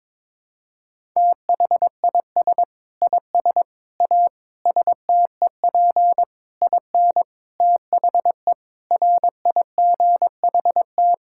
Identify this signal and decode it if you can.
Morse code, heard as a keyed tone, about 22 wpm, 700 Hz. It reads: THIS IS A STEP IN THE RIGHT
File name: practice / 22wpm / Sets of 8 Words - 22wpm